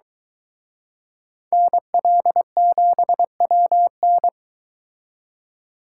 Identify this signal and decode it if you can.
Morse code, heard as a keyed tone, about 23 wpm, 700 Hz. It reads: NL7WN